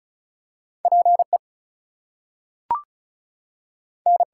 Morse code, heard as a keyed tone, about 36 wpm, 700 Hz. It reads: PE N
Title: US States and Canadian Provinces 35wpm